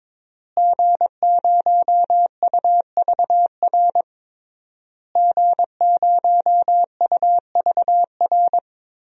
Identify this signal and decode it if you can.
Morse code, heard as a keyed tone, about 21 wpm, 700 Hz. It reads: G0UVR G0UVR
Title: DXCC-100 Call Signs - 5 Characters - Repeated 3x - 22wpm